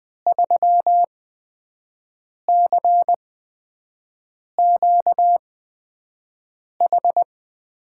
Morse code, heard as a keyed tone, about 20 wpm, spaced as for 5 wpm, 700 Hz. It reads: 3CQH
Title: USA Call Signs - 8x Character Spacing - 20wpm